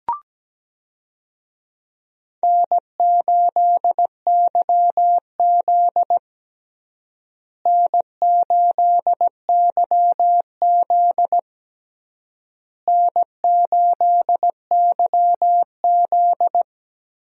Morse code, heard as a keyed tone, about 17 wpm, 700 Hz. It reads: N8YZ N8YZ N8YZ
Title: USA Call Signs - 4 Characters - Repeated 3x - 17wpm